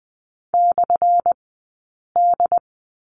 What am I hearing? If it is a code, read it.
Morse code, heard as a keyed tone, about 20 wpm, spaced as for 8 wpm, 700 Hz. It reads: /D
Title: 190 Lesson 31d - Letter v - Two Characters 20wpm